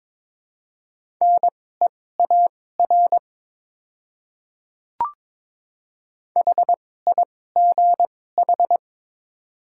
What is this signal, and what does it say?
Morse code, heard as a keyed tone, about 22 wpm, spaced as for 16 wpm, 700 Hz. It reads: NEAR HIGH
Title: 2x Character Spacing - Top 300 Words - 22wpm